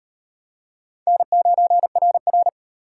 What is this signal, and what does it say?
Morse code, heard as a keyed tone, about 38 wpm, 700 Hz. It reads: N9RR